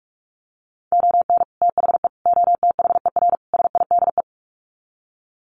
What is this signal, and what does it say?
Morse code, heard as a keyed tone, about 44 wpm, 700 Hz. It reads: ON THE OTHER SIDE
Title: Sets of 4 Words 45wpm